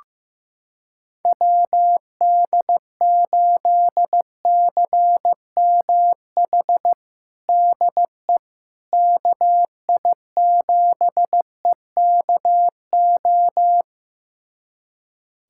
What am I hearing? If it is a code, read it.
Morse code, heard as a keyed tone, about 15 wpm, 700 Hz. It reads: WD8CMH DE KI7EKO
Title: USA Callsign DE Callsign 15wpm